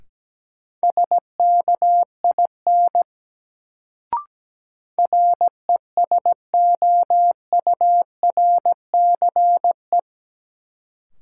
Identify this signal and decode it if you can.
Morse code, heard as a keyed tone, about 17 wpm, 700 Hz. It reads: SKIN RESOURCE
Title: Sentences from Top 1300 Words - New Word Drill - 17wpm